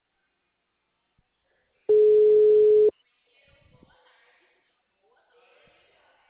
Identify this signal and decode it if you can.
Ringback tone (Europe, 425 Hz).
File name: out- -2014- - - .86923